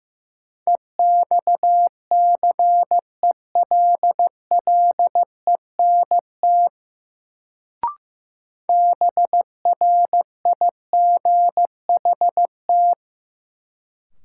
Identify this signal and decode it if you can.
Morse code, heard as a keyed tone, about 16 wpm, 700 Hz. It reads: EXCELLENT BRIGHT
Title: Sentences from Top 1000 Words - New Word Drill - 15wpm